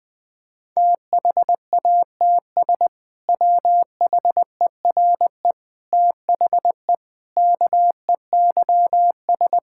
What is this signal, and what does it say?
Morse code, heard as a keyed tone, about 20 wpm, 700 Hz. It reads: THATS WHERE THE KEYS
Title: Sentences with 2-Word Phrases - Spoken Beginning - 20wpm